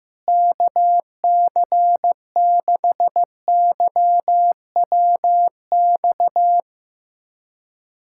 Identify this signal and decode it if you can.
Morse code, heard as a keyed tone, about 15 wpm, 700 Hz. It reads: KC6YWX
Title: USA Callsign DE Callsign 15wpm